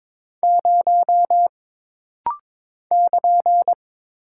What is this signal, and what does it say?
Morse code, heard as a keyed tone, about 22 wpm, 700 Hz. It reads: 0 <KN>
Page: This rapid-fire timing is geared to push your ICR (Instant Character Recognition) skills.